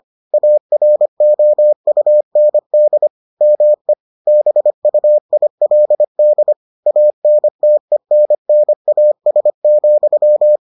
Morse code, heard as a keyed tone, about 25 wpm, 600 Hz. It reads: AROUND ME BUILD ANTENNAS,